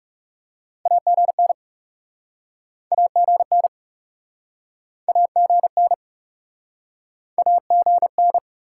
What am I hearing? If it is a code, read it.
Morse code, heard as a keyed tone, about 44 wpm, 700 Hz. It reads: AGN AGN AGN AGN